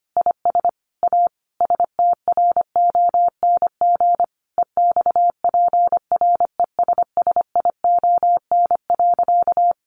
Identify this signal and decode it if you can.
Morse code, heard as a keyed tone, about 25 wpm, 700 Hz. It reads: IS A STRONG EXPRESSION.
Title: Sentences from Top 1200 Words 25wpm